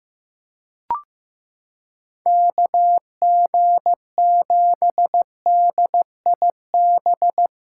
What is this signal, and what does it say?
Morse code, heard as a keyed tone, about 15 wpm, 700 Hz. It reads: KG7DIB
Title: USA Callsign DE Callsign 15wpm